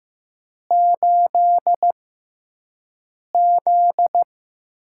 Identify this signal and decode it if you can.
Morse code, heard as a keyed tone, about 15 wpm, spaced as for 5 wpm, 700 Hz. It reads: 8Z
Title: USA Call Signs - 4 Characters - 6x Character Spacing - 15wpm